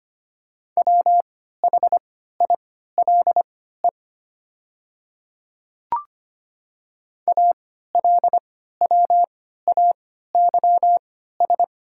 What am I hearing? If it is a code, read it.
Morse code, heard as a keyed tone, about 25 wpm, spaced as for 14 wpm, 700 Hz. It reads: WHILE ALWAYS
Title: 3x Character Spacing - Top 500 Words - 25wpm